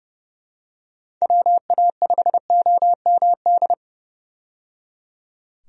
Morse code, heard as a keyed tone, about 30 wpm, 700 Hz. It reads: WA5OMD
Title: USA Call Signs - 6 Characters - 30wpm